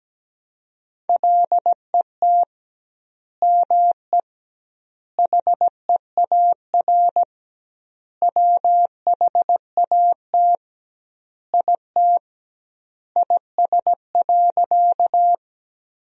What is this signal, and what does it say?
Morse code, heard as a keyed tone, about 17 wpm, 700 Hz. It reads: LET ME HEAR WHAT IT IS.
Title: 2x Word-Spacing - Sentences from Top 300 Words - 17wpm